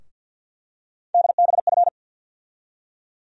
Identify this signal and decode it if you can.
Morse code, heard as a keyed tone, about 52 wpm, 700 Hz. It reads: DBF
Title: USA Call Sign Postfixes - 3 Letters 50wpm